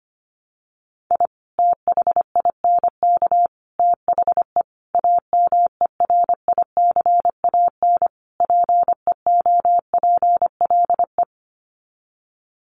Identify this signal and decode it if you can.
Morse code, heard as a keyed tone, about 25 wpm, 690 Hz. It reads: I THINK THE AMERICAN PEOPLE
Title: Sets of 5 Words Encore 25wpm